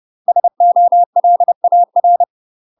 Morse code, heard as a keyed tone, about 30 wpm, 700 Hz. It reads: SOLAR
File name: UK MASTER QSOs TRAINING / 30WPM / UK MASTER QSOs 0019